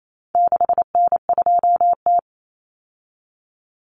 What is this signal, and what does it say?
Morse code, heard as a keyed tone, about 28 wpm, 700 Hz. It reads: 6N2T
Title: Common Cut Numbers - 4-Digit 28wpm